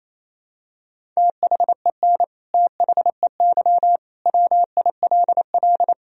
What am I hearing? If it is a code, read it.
Morse code, heard as a keyed tone, about 28 wpm, 700 Hz. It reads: THEN THEY WILL
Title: Sentences from Top 200 Words and Abbreviations 28wpm